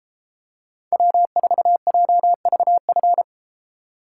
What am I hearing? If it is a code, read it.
Morse code, heard as a keyed tone, about 32 wpm, 700 Hz. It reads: W4JVF